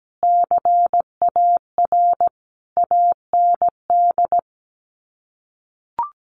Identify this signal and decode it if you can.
Morse code, heard as a keyed tone, about 17 wpm, 700 Hz. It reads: CAR AND
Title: Sets of 5 Words Encore 17wpm